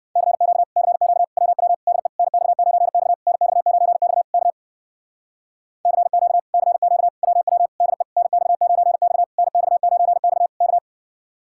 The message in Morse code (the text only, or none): CQ CQ WW DE NX9X NX9X K CQ CQ WW DE NX9X NX9X K